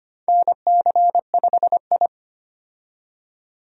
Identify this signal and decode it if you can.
Morse code, heard as a keyed tone, about 25 wpm, 700 Hz. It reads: NC5I
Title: USA Call Signs - 4 Characters - 25wpm